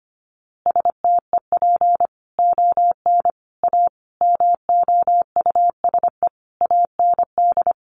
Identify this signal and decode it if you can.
Morse code, heard as a keyed tone, about 25 wpm, 700 Hz. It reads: STEP ON A MOUSE AND
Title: A Sound Of Thunder 25wpm